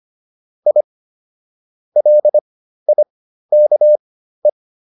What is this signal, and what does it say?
Morse code, heard as a keyed tone, about 25 wpm, spaced as for 13 wpm, 600 Hz. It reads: I LIKE